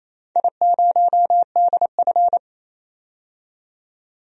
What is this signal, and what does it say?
Morse code, heard as a keyed tone, about 29 wpm, 700 Hz. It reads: I0DF
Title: DXCC-100 Call Signs - 4 Characters - 28wpm